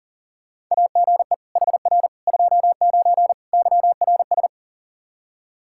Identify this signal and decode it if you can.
Morse code, heard as a keyed tone, about 40 wpm, 690 Hz. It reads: AGE HR 29 YRS
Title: Practice QSO - Exchange 3 - More Info - 40wpm